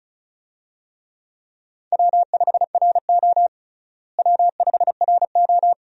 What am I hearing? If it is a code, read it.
Morse code, heard as a keyed tone, about 36 wpm, 700 Hz. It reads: W5RO W5RO
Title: USA Call Signs - 4 Characters - Repeated 2x - 35wpm